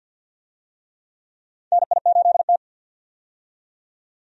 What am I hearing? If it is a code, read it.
Morse code, heard as a keyed tone, about 51 wpm, 700 Hz. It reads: NI8T
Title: USA Call Signs - 4 Characters - 50wpm